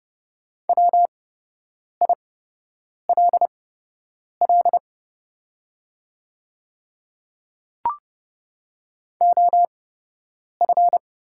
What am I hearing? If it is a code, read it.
Morse code, heard as a keyed tone, about 30 wpm, spaced as for 8 wpm, 690 Hz. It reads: WILL OF
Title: 8x Character Spacing - Top 200 Words - 30wpm